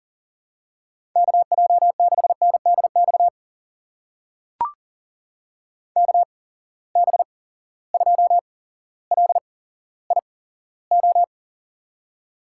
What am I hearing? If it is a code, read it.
Morse code, heard as a keyed tone, about 42 wpm, spaced as for 17 wpm, 700 Hz. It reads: KJ6NDX KB2LIO
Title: USA Call Signs - 6 Characters - 8x Character Spacing - 40wpm